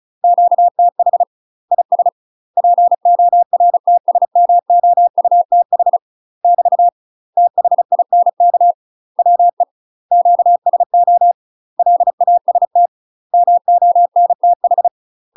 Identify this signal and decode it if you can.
Morse code, heard as a keyed tone, about 34 wpm, 700 Hz. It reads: QTH IS PORTSMOUTH <BT> THINK WE QSO LAST MONTH
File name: UK MASTER QSOs TRAINING / 35WPM / UK MASTER QSOs 0005